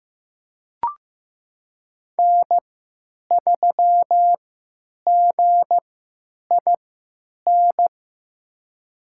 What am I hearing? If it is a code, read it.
Morse code, heard as a keyed tone, about 15 wpm, spaced as for 9 wpm, 700 Hz. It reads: N3GIN
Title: USA Call Signs - 3x Character Spacing - 15wpm